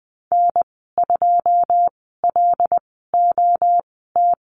Morse code, heard as a keyed tone, about 20 wpm, spaced as for 15 wpm, 700 Hz. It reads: N2LOT
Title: 088 Lesson 15g - Number 5 - Call Signs 20wpm